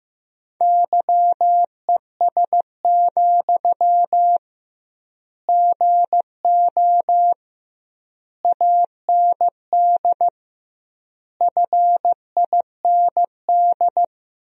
Morse code, heard as a keyed tone, about 15 wpm, 700 Hz. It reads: YES, GO AND FIND
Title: 2x Word-Spacing - Sentences from Top 500 Words - 15wpm